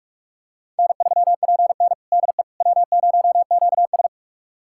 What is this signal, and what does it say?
Morse code, heard as a keyed tone, about 46 wpm, 700 Hz. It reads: N3PN DE W0QS
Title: USA Callsign DE Callsign - 4 Characters - 45wpm